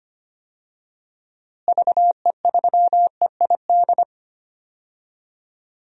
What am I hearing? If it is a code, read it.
Morse code, heard as a keyed tone, about 25 wpm, 700 Hz. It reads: VE3EID
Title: DXCC-100 Call Signs - 6 Characters - Repeated 3x - 25wpm